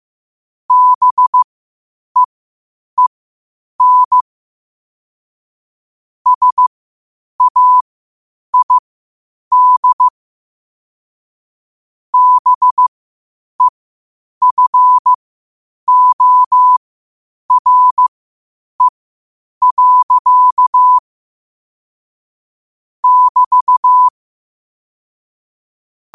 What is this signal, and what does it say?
Morse code, heard as a keyed tone, about 15 wpm, spaced as for 8 wpm, 1000 Hz. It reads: BEEN SAID BEFORE. <BT>
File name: qotdfiles / 8wpm